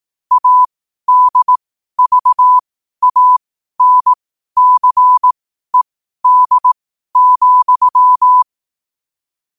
Morse code, heard as a keyed tone, about 18 wpm, spaced as for 12 wpm, 1000 Hz. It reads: ADVANCED,
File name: qotdfiles / 12wpm